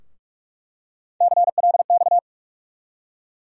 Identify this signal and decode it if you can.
Morse code, heard as a keyed tone, about 41 wpm, 700 Hz. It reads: CLX
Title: USA Call Sign Postfixes - 3 Letters 45wpm